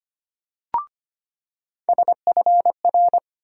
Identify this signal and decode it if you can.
Morse code, heard as a keyed tone, about 25 wpm, 700 Hz. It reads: SFR
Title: USA Call Sign Postfixes - 3 Letters 25wpm